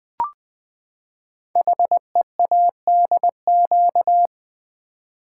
Morse code, heard as a keyed tone, about 20 wpm, 700 Hz. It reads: HEADQ
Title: Build A Word - 5 Letters At A Time - 12-Letter Words - 20wpm